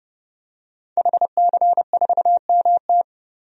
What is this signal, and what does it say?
Morse code, heard as a keyed tone, about 30 wpm, 700 Hz. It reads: HC4MT
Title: DXCC-100 Call Signs - 5 Characters - 30wpm